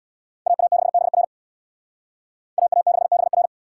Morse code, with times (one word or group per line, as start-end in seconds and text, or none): 0.5-1.2 AA6BR
2.6-3.5 AA6BR